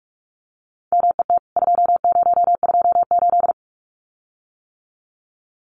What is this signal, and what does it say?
Morse code, heard as a keyed tone, about 44 wpm, 700 Hz. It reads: MET 2028
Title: CWT Exchange 45wpm